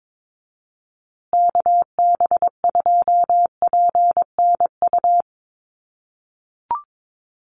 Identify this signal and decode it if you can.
Morse code, heard as a keyed tone, about 22 wpm, 700 Hz. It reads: KB2PNU
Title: USA Call Signs Repeated 2x 22wpm